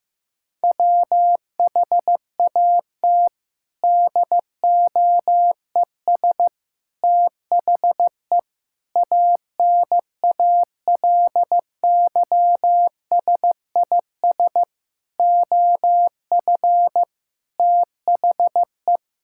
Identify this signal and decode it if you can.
Morse code, heard as a keyed tone, about 14 wpm, 700 Hz. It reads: WHAT DOES THE ANALYSIS OF THE
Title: Sentences from Top 1000 Words - Review Part 6 - 15wpm